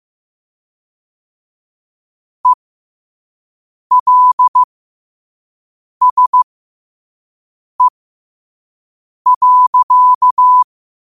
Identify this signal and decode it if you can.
Morse code, heard as a keyed tone, about 15 wpm, spaced as for 5 wpm, 1000 Hz. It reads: ELSE.
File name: qotdfiles / 5wpm